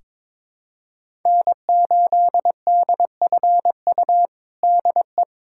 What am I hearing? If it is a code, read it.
Morse code, heard as a keyed tone, about 22 wpm, 700 Hz. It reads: N8DFU DE
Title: USA Callsign DE Callsign - 5 Characters - 22wpm